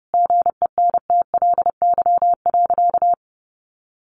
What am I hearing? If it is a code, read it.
Morse code, heard as a keyed tone, about 30 wpm, 700 Hz. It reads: GENTLY.
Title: Sentences from Top 1500 Words 30wpm